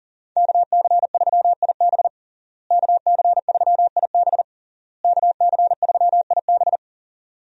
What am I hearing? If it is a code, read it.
Morse code, heard as a keyed tone, about 40 wpm, 700 Hz. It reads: KC3IB KC3IB KC3IB
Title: USA Call Signs - 5 Characters - Repeated 3x - 40wpm